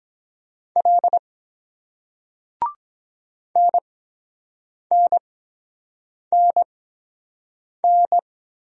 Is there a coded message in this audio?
Yes: Morse code from 0.8 to 8.2 s, heard as a keyed tone, about 23 wpm, spaced as for 7 wpm, 700 Hz. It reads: L NNNN